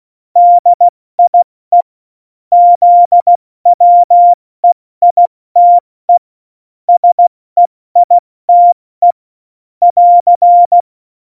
Morse code, heard as a keyed tone, about 16 wpm, 700 Hz. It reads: DIE ZWEITE SEITE <AR>
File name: Morsekurs / Witze / Witz 17. Oktober 17, Tempo 70